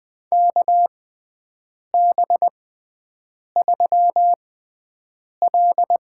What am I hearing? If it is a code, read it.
Morse code, heard as a keyed tone, about 20 wpm, spaced as for 7 wpm, 700 Hz. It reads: KB3L
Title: USA Call Signs - 6 Characters - 6x Character Spacing - 20wpm